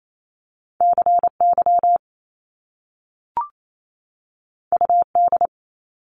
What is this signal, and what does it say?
Morse code, heard as a keyed tone, about 28 wpm, 700 Hz. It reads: CY UD
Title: Top Two Letter Combinations in Words 28wpm